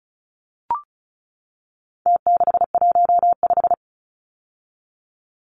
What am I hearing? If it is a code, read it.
Morse code, heard as a keyed tone, about 34 wpm, 700 Hz. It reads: T615
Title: Numbers - Contest Style - 4-Digit 35wpm